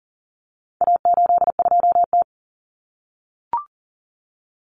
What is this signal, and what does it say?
Morse code, heard as a keyed tone, about 40 wpm, 700 Hz. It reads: A82T
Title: Common Cut Numbers - 4-Digit 40wpm